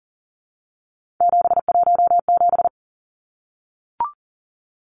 Morse code, heard as a keyed tone, about 38 wpm, 700 Hz. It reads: 717
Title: Numbers - 3-Digit 40wpm